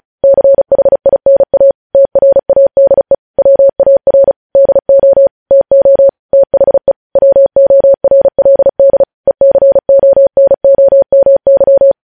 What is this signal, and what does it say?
Morse code, heard as a keyed tone, about 36 wpm, 550 Hz. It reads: CHINA TRADE WAR DO TO THE WORLD ECONOMY
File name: podcast / News headlines-1x-35x35